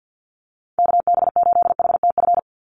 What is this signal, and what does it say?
Morse code, heard as a keyed tone, about 50 wpm, 700 Hz. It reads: KB8HTF
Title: USA Call Signs 50wpm